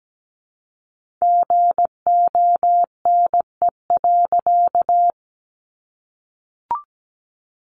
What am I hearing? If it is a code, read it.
Morse code, heard as a keyed tone, about 17 wpm, 700 Hz. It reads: GONE.
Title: A Sound Of Thunder 17wpm